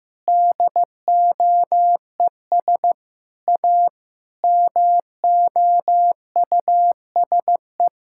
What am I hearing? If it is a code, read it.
Morse code, heard as a keyed tone, about 16 wpm, 700 Hz. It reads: DOES A MOUSE
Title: Sentences from Top 1000 Words - Review Part 5 - 15wpm